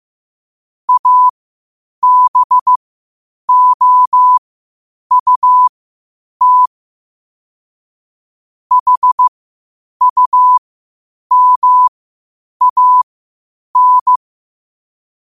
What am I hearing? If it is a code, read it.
Morse code, heard as a keyed tone, about 15 wpm, spaced as for 8 wpm, 1000 Hz. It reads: ABOUT HUMAN
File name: qotdfiles / 8wpm